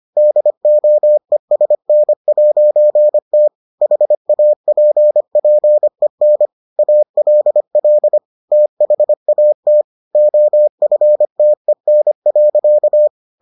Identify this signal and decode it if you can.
Morse code, heard as a keyed tone, about 25 wpm, 600 Hz. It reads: DOESN'T HAPPEN ALL THAT OFTEN.